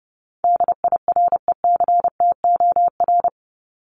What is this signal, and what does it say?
Morse code, heard as a keyed tone, about 30 wpm, 700 Hz. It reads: DIRECTOR
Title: Top 1251-1500 Words 30wpm